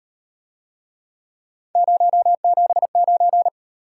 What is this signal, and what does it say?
Morse code, heard as a keyed tone, about 38 wpm, 700 Hz. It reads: 079